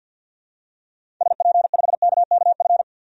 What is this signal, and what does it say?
Morse code, heard as a keyed tone, about 46 wpm, 700 Hz. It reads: SP5KKF
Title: DXCC-100 Call Signs - 6 Characters - 50wpm